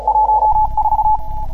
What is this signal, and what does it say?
Morse code, heard as a keyed tone, about 38 wpm, 810 Hz. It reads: 14